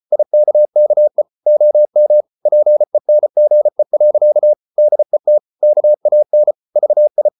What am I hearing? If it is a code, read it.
Morse code, heard as a keyed tone, about 34 wpm, 600 Hz. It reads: IKKE OM PENGE. DET KAN VI
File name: Ferie QRQ 34wpm